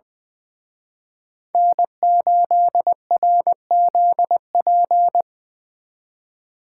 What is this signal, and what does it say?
Morse code, heard as a keyed tone, about 24 wpm, 700 Hz. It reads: N8RZP
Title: Speed Racing - USA 1x3 Call Signs 20wpm